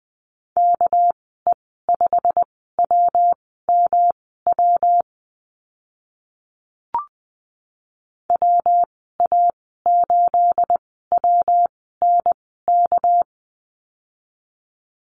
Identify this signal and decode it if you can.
Morse code, heard as a keyed tone, about 20 wpm, spaced as for 14 wpm, 700 Hz. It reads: KE5WMW WA8WNK
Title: 213 Lesson 34g - Number 8 - Call Signs 20wpm